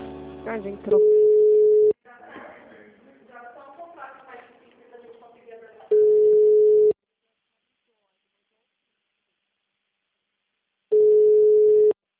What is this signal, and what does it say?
Ringback tone (Europe, 425 Hz), with speech.